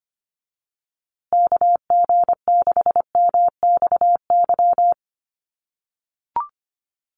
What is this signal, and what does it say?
Morse code, heard as a keyed tone, about 25 wpm, 700 Hz. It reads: KG6MXY
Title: USA Call Signs Repeated 3x 25wpm